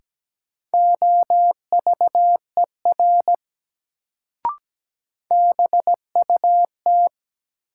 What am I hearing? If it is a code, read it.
Morse code, heard as a keyed tone, about 17 wpm, 700 Hz. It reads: OVER BUT
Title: Sentences from Top 100 Words - New Word Drill - 17wpm